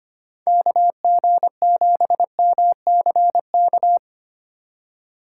Morse code, heard as a keyed tone, about 30 wpm, 700 Hz. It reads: KG7MCK
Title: Speed Racing - USA 2x3 Call Signs 25wpm